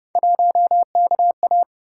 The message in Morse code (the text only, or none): 1KA